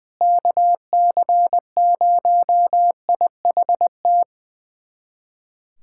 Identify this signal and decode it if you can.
Morse code, heard as a keyed tone, about 20 wpm, 700 Hz. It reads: KC0IHT
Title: USA Callsign DE Callsign - 6 Characters - 20wpm